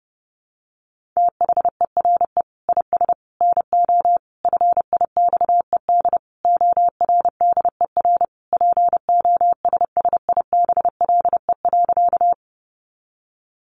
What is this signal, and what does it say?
Morse code, heard as a keyed tone, about 28 wpm, 700 Hz. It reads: THERE IS NO FIXED ORDER POSSIBLE.
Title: Sentences from Top 1100 Words 30wpm